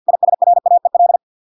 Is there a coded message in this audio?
Yes: Morse code from 0.1 to 1.2 s, heard as a keyed tone, about 50 wpm, 700 Hz. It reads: ISRAEL